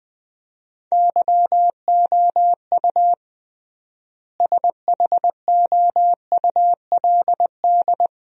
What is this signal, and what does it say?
Morse code, heard as a keyed tone, about 21 wpm, 700 Hz. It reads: YOU SHOULD
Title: 3x Word-Spacing - Sentences from Top 400 Words - 20wpm